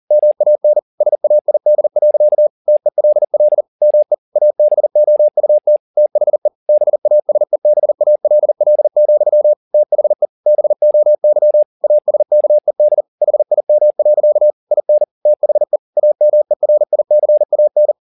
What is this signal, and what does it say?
Morse code, heard as a keyed tone, about 40 wpm, 600 Hz. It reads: MAN SAID. TELL ME ABOUT THE BASEBALL, THE BOY ASKED HIM. IN THE AMERICAN